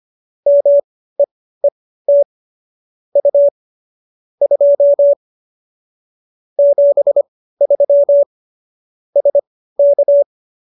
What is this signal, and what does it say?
Morse code, heard as a keyed tone, about 25 wpm, spaced as for 15 wpm, 570 Hz. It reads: MEET U 2 73 SK